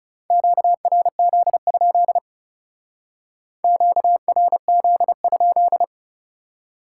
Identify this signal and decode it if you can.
Morse code, heard as a keyed tone, about 36 wpm, 700 Hz. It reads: QRZ? QRZ?
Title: Speed-Racing Top 90 QSO Elements 30wpm